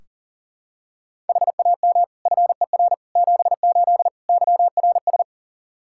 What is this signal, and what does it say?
Morse code, heard as a keyed tone, about 40 wpm, 700 Hz. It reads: HAM FER 78 YRS
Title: Practice QSO - Exchange 3 - More Info - 40wpm